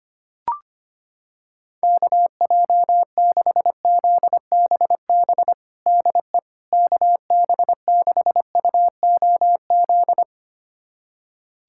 Morse code, heard as a keyed tone, about 25 wpm, 700 Hz. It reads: KJ6ZBB DE KB6UOZ
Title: USA Callsign DE Callsign 25wpm